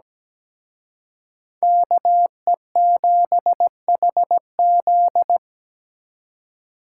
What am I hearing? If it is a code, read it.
Morse code, heard as a keyed tone, about 20 wpm, 700 Hz. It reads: KE7HZ
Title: Speed Racing - USA 2x2 Call Signs 17wpm